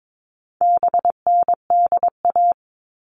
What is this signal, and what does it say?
Morse code, heard as a keyed tone, about 22 wpm, 700 Hz. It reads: BNDA
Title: All Cut Numbers - 4-Digit 22wpm